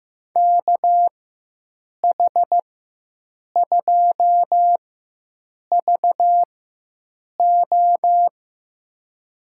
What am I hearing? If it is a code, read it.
Morse code, heard as a keyed tone, about 15 wpm, spaced as for 7 wpm, 700 Hz. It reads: KH2VO
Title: USA Call Signs - 4x Character Spacing - 15wpm